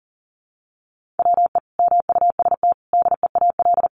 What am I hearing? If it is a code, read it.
Morse code, heard as a keyed tone, about 40 wpm, 700 Hz. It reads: WE MUST DEAL
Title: Sentences from Top 1400 Words 40wpm